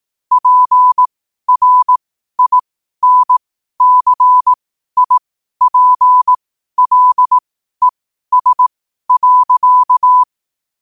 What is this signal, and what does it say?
Morse code, heard as a keyed tone, about 18 wpm, spaced as for 12 wpm, 1000 Hz. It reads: PRINCIPLES.
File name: qotdfiles / 12wpm